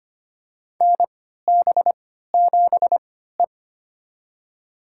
Morse code, heard as a keyed tone, about 25 wpm, spaced as for 14 wpm, 700 Hz. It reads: NB7E
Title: USA Call Signs - 4 Characters - 3x Character Spacing - 25wpm